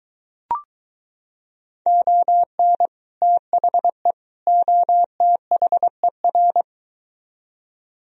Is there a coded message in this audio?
Yes: Morse code from 1.9 to 6.6 s, heard as a keyed tone, about 23 wpm, 700 Hz. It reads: ON THE OTHER